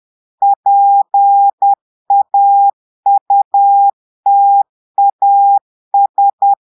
Morse code, heard as a keyed tone, about 10 wpm, 800 Hz. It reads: PAUTAS
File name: portugais texte en clair 2 vitesse 10 mots minutes